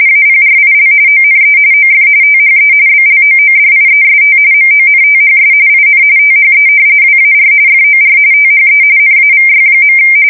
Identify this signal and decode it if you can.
RTTY, about 45.45 baud. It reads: APRIL ONE OF THE BLOODIEST MONTHS IN THE WAR FOR THE NAVY.
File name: fsk170 / battleshipnj